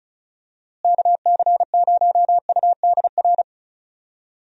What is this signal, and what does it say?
Morse code, heard as a keyed tone, about 36 wpm, 700 Hz. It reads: KC0UDR
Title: USA Call Signs - 6 Characters - Repeated 2x - 35wpm